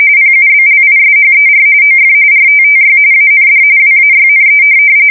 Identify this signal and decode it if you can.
RTTY, about 45.45 baud. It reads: PUBLISHER OF RTTY AND WAYNE
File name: fsk170 / teletype-RTTY